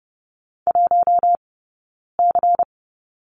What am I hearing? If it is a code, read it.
Morse code, heard as a keyed tone, about 30 wpm, 700 Hz. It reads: 1 C
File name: Lesson 18d - Letter m - Two Characters 30wpm